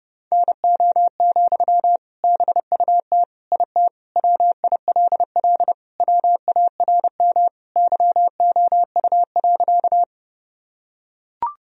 Morse code, heard as a keyed tone, about 30 wpm, 700 Hz. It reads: NO, BUT IT WILL WARM YOU.
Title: Sentences from Top 1000 Words - Review Part 2 - 30wpm